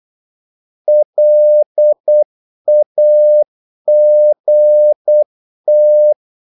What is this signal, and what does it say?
Morse code, heard as a keyed tone, about 8 wpm, 600 Hz. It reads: LAGT
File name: Korte ord DK 8wpm